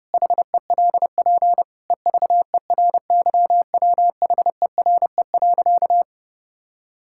Morse code, heard as a keyed tone, about 32 wpm, 700 Hz. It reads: HELP EVERYWHERE.
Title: Sentences from Top 1600 Words 30wpm